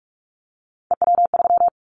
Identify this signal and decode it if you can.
Morse code, heard as a keyed tone, about 44 wpm, 700 Hz. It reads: EW3
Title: Callsign Prefixes for Top 100 DXCC - 45wpm